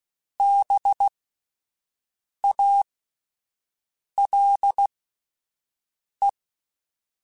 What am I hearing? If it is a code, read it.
Morse code, heard as a keyed tone, about 16 wpm, spaced as for 5 wpm, 800 Hz. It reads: BALE